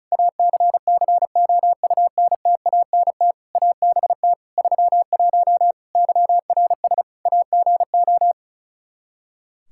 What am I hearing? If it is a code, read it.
Morse code, heard as a keyed tone, about 36 wpm, 700 Hz. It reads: ACCOUNTANT ABT 31 YRS AGO
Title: K1USN SST QSO Practice 35wpm